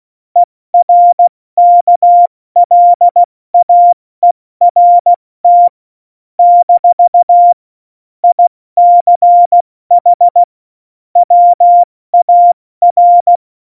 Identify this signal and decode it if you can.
Morse code, heard as a keyed tone, about 16 wpm, 700 Hz. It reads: ERKLAERT - ICH WAR
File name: Morsekurs / Witze / Witz 29. Dezember 2020, 70 F80